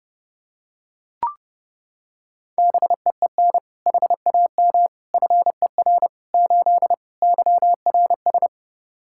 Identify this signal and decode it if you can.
Morse code, heard as a keyed tone, about 32 wpm, 700 Hz. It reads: BEEN HAM FER 8 YRS
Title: Practice QSO - Exchange 3 - More Info - 30wpm